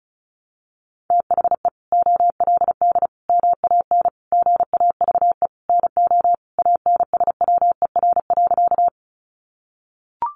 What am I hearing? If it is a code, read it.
Morse code, heard as a keyed tone, about 34 wpm, 700 Hz. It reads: THE OLD MAN GAVE NO ANSWER.
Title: Sentences from Top 500 Words 35wpm